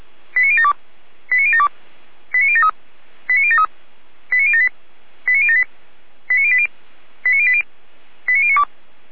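Five-tone selective call (ZVEI), heard as quick runs of five tones, tones of about 70 ms. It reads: 89072 89072 89073 89073 89078 89078 89080 89080 89092
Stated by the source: ZVEI five-tone selective call received by SDR.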